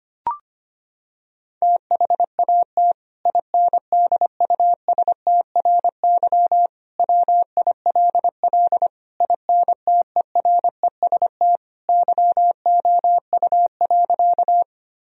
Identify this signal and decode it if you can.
Morse code, heard as a keyed tone, about 25 wpm, 700 Hz. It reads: THAT INDUSTRY WILL INTEREST YOU.
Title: Sentences from Top 1000 Words - Review Part 4 - 25wpm